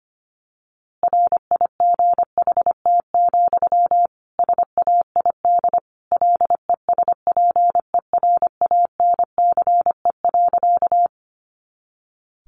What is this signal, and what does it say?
Morse code, heard as a keyed tone, about 25 wpm, 700 Hz. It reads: RIGHT, SAID LESPERANCE.
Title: A Sound Of Thunder 25wpm